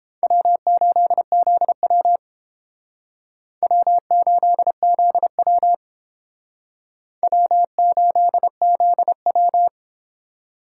Morse code, heard as a keyed tone, about 29 wpm, 700 Hz. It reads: W8ZW W8ZW W8ZW